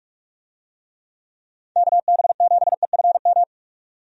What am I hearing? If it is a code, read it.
Morse code, heard as a keyed tone, about 42 wpm, 700 Hz. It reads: KB7EFM
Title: USA Call Signs - 6 Characters - 45wpm